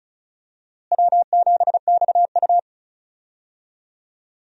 Morse code, heard as a keyed tone, about 34 wpm, 700 Hz. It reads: W7XU
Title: USA Call Signs - 4 Characters - Repeated 2x - 35wpm